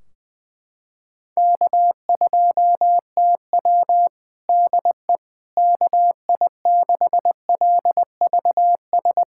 Morse code, heard as a keyed tone, about 20 wpm, 700 Hz. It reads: K2TW DE KI6LVS
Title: Practice QSO - Exchange 4 - Ending - 20wpm